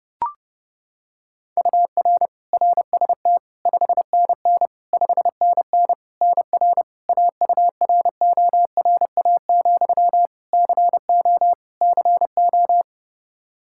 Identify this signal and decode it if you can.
Morse code, heard as a keyed tone, about 30 wpm, 700 Hz. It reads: UR RST 5NN 5NN NR AURORA, CO CO
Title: Practice QSO - Exchange 1 - Beginning - 30wpm